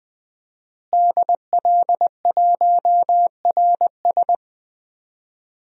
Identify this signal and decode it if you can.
Morse code, heard as a keyed tone, about 20 wpm, 700 Hz. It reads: DL1RS
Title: DXCC-100 Call Signs - 5 Characters - Repeated 3x - 20wpm